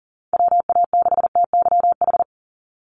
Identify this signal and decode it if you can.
Morse code, heard as a keyed tone, about 41 wpm, 700 Hz. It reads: WA6TYH
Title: USA Call Signs 40wpm